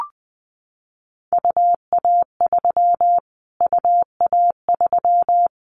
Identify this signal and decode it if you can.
Morse code, heard as a keyed tone, about 20 wpm, 700 Hz. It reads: UA3 UA3
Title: Callsign Prefixes for Top 25 DXCC - Repeated 3x - 20wpm